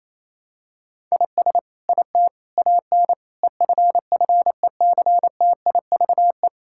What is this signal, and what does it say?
Morse code, heard as a keyed tone, about 28 wpm, 700 Hz. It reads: IS IT AN EFFECTIVE
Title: Sentences from Top 1000 Words - Review Part 6 - 28wpm